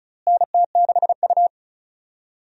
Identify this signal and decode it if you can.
Morse code, heard as a keyed tone, about 36 wpm, 700 Hz. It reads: NT6U
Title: USA Call Signs - 4 Characters - Repeated 2x - 35wpm